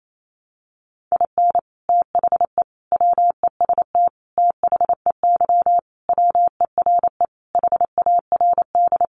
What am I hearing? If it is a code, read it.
Morse code, heard as a keyed tone, about 28 wpm, 700 Hz. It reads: IN THE WEST THEY WERE HARD
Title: Sentences from Top 1300 Words 28wpm